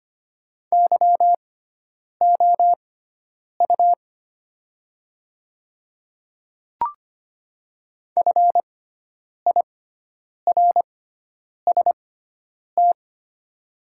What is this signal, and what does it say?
Morse code, heard as a keyed tone, about 25 wpm, spaced as for 9 wpm, 690 Hz. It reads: YOU FIRST
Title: 6x Character Spacing - Top 100 Words - 25wpm